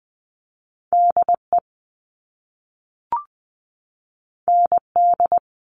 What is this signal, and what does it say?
Morse code, heard as a keyed tone, about 20 wpm, 700 Hz. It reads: DE ND
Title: All Cut Numbers - 2-Digit 20wpm